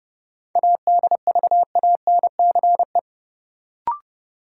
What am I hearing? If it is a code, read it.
Morse code, heard as a keyed tone, about 30 wpm, 700 Hz. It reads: ADVANCE
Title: Sentences from Top 1400 Words - New Word Drill Alt - 30wpm